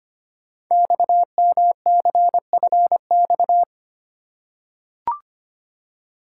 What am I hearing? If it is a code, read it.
Morse code, heard as a keyed tone, about 25 wpm, 700 Hz. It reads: XMCFX
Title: Code Groups - 5 Letters - 25wpm